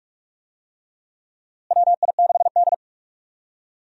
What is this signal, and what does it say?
Morse code, heard as a keyed tone, about 42 wpm, 700 Hz. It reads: WI6D
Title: USA Call Signs - 4 Characters - Repeated 3x - 45wpm